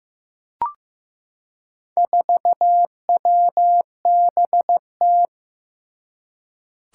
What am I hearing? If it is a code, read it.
Morse code, heard as a keyed tone, about 15 wpm, 700 Hz. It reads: 4WBT
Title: Four Challenging Letter-Number Combinations 15wpm